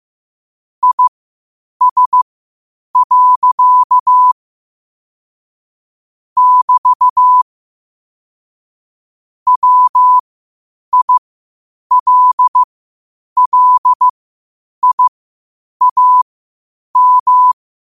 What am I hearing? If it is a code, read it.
Morse code, heard as a keyed tone, about 15 wpm, spaced as for 8 wpm, 1000 Hz. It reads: IS. <BT> WILLIAM
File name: qotdfiles / 8wpm